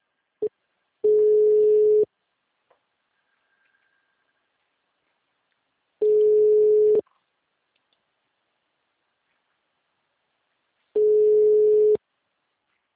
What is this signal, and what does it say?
Ringback tone (Europe, 425 Hz).